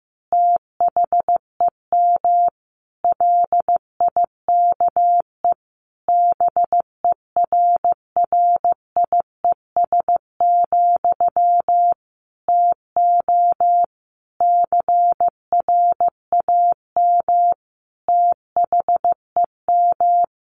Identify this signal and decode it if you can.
Morse code, heard as a keyed tone, about 15 wpm, 700 Hz. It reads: THEM LIKE BERRIES, TO CRAM THEM
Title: A Sound Of Thunder 15wpm